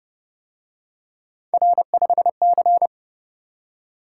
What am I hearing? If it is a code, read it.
Morse code, heard as a keyed tone, about 29 wpm, 700 Hz. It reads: R5C